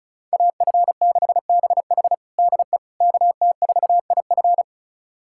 Morse code, heard as a keyed tone, about 36 wpm, 700 Hz. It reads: AF6BH DE KT4IF
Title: USA Callsign DE Callsign - 5 Characters - 35wpm